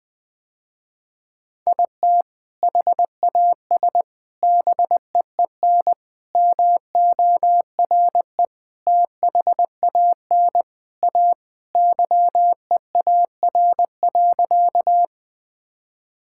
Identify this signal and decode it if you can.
Morse code, heard as a keyed tone, about 20 wpm, 700 Hz. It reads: IT HAS BEEN MORE THAN A YEAR.
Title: Sentences with 4-Word Phrases - Spoken Beginning - 20wpm